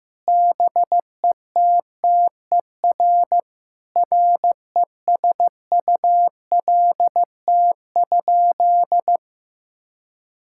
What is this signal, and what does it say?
Morse code, heard as a keyed tone, about 15 wpm, 700 Hz. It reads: BETTER RESULT?
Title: Sentences from Top 1000 Words - Review Part 6 - 15wpm